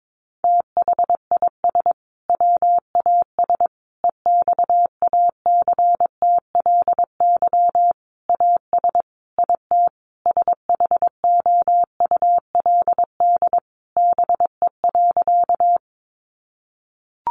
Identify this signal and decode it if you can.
Morse code, heard as a keyed tone, about 22 wpm, 700 Hz. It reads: THIS WAS EXACTLY AS IT SHOULD BE.
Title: Sentences from Top 900 Words 22wpm